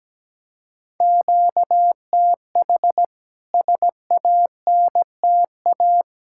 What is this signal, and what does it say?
Morse code, heard as a keyed tone, about 17 wpm, 700 Hz. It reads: QTH SANTA
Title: K1USN SST QSO Practice 17wpm